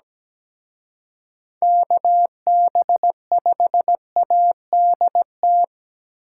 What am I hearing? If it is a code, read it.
Morse code, heard as a keyed tone, about 20 wpm, 700 Hz. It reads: KB5ADT
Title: Speed Racing - USA 2x3 Call Signs 17wpm